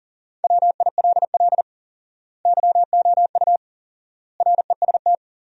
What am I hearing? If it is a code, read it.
Morse code, heard as a keyed tone, about 40 wpm, 700 Hz. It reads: WILL YOU REST
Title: 4x Word-Spacing - Sentences from Top 500 Words - 40wpm